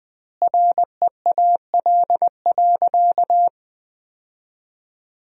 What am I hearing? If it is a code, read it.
Morse code, heard as a keyed tone, about 21 wpm, 700 Hz. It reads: REAL.
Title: 2x Word-Spacing - Sentences from Top 300 Words - 20wpm